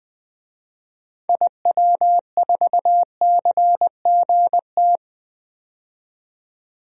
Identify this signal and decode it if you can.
Morse code, heard as a keyed tone, about 20 wpm, 700 Hz. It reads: IW4CGT
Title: DXCC-100 Call Signs - 6 Characters - 20wpm